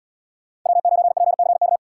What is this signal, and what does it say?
Morse code, heard as a keyed tone, about 67 wpm, 700 Hz. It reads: W9PCK